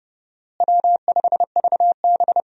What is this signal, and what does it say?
Morse code, heard as a keyed tone, about 30 wpm, 700 Hz. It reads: W5VB
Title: USA Call Signs - 4 Characters - Repeated 2x - 30wpm